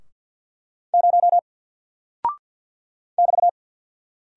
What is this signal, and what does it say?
Morse code, heard as a keyed tone, about 48 wpm, 700 Hz. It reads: 0 <BT>
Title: Single Character (Letter:Number:Punctuation:Prosign) 50wpm